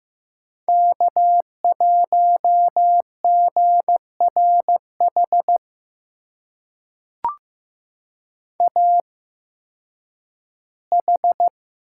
Morse code, heard as a keyed tone, about 15 wpm, spaced as for 10 wpm, 700 Hz. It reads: K1GRH AH
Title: USA Call Signs - 8x Character Spacing - 15wpm